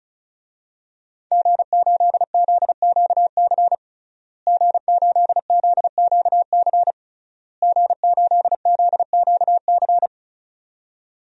Morse code, heard as a keyed tone, about 35 wpm, 700 Hz. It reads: G8ZQC G8ZQC G8ZQC
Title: DXCC-100 Call Signs - 5 Characters - Repeated 3x - 35wpm